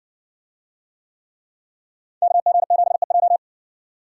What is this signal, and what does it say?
Morse code, heard as a keyed tone, about 60 wpm, 700 Hz. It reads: KG7EJ